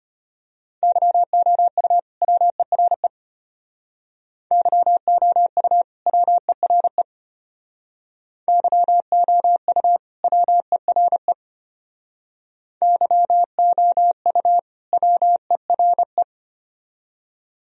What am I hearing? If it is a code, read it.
Morse code, heard as a keyed tone, about 33 wpm, 700 Hz. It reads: YOU WERE YOU WERE YOU WERE YOU WERE